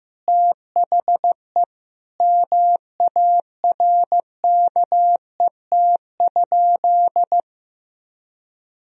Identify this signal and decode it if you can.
Morse code, heard as a keyed tone, about 15 wpm, 700 Hz. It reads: THE MARKET?
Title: Sentences from Top 1000 Words - Review Part 3 - 15wpm